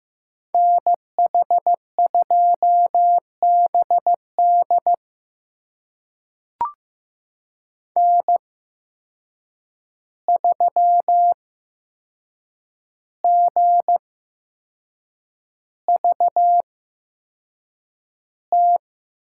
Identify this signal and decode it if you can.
Morse code, heard as a keyed tone, about 15 wpm, spaced as for 6 wpm, 700 Hz. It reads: NH2BD N3GVT
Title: USA Call Signs - 5 Characters - 8x Character Spacing - 15wpm